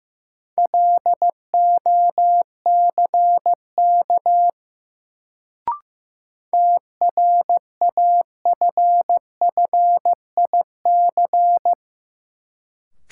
Morse code, heard as a keyed tone, about 15 wpm, 700 Hz. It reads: LOCK TRAFFIC
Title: Sentences from Top 1300 Words - New Word Drill - 15wpm